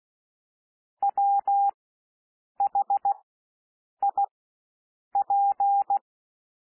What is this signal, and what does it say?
Morse code, heard as a keyed tone, about 16 wpm, spaced as for 7 wpm, 800 Hz. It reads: WHIP
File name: SOUND WORDS 16 wpm